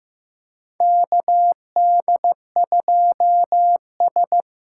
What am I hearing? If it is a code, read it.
Morse code, heard as a keyed tone, about 15 wpm, 700 Hz. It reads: KD2S
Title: USA Call Signs - 6 Characters - 8x Character Spacing - 15wpm